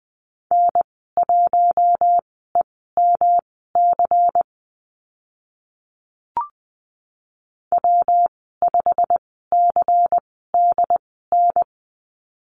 Morse code, heard as a keyed tone, about 20 wpm, spaced as for 15 wpm, 700 Hz. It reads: N1EMC W5CDN
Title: 154 Lesson 25g - Letter p - Call Signs 20wpm